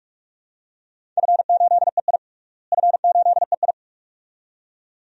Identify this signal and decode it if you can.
Morse code, heard as a keyed tone, about 46 wpm, 700 Hz. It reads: F8EI F8EI
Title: DXCC-100 Call Signs - 4 Characters - Repeated 2x - 45wpm